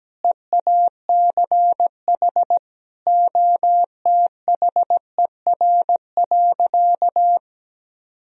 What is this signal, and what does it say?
Morse code, heard as a keyed tone, about 17 wpm, 700 Hz. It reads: EACH OTHER.
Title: Sentences from Top 1000 Words - Review Part 1 - 17wpm